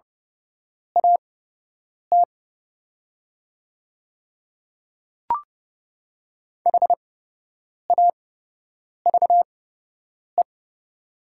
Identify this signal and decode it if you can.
Morse code, heard as a keyed tone, about 30 wpm, spaced as for 8 wpm, 700 Hz. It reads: AT HAVE
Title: 8x Character Spacing - Top 300 Words - 30wpm